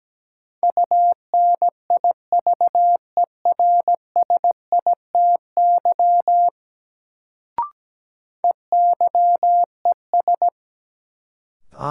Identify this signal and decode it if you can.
Morse code, heard as a keyed tone, about 17 wpm, 700 Hz. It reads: UNIVERSITY EYES
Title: Sentences from Top 1000 Words - Review Part 4 - New Word Drill - 17wpm